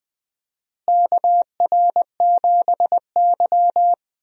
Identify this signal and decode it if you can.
Morse code, heard as a keyed tone, about 20 wpm, 700 Hz. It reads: KR7Y
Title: USA Call Signs - 4 Characters - Repeated 3x - 20wpm